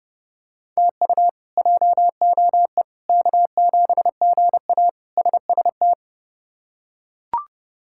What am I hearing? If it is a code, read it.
Morse code, heard as a keyed tone, about 30 wpm, 700 Hz. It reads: TU JOE K7GA SST
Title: K1USN SST QSO Practice - 30wpm